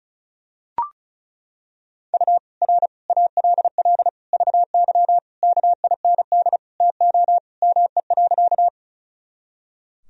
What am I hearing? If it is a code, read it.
Morse code, heard as a keyed tone, about 36 wpm, 700 Hz. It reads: U R ALL VY KIND TO ME.
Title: Sentences from Top 200 Words and Abbreviations 35wpm